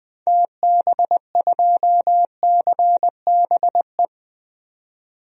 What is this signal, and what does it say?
Morse code, heard as a keyed tone, about 20 wpm, 700 Hz. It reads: TB2CBE
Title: DXCC-100 Call Signs - 6 Characters - 20wpm